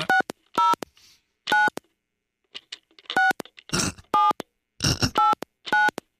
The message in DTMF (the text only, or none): 60 9 6 7 89